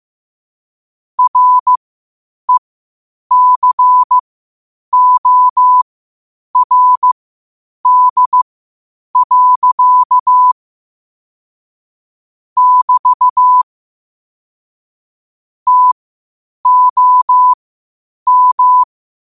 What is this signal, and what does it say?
Morse code, heard as a keyed tone, about 15 wpm, spaced as for 8 wpm, 1000 Hz. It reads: RECORD. <BT> TOM
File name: qotdfiles / 8wpm